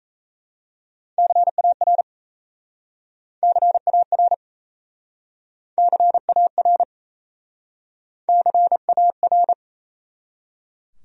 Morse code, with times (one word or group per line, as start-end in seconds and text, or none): 1.2-2.0 CAR
3.4-4.3 CAR
5.8-6.8 CAR
8.3-9.5 CAR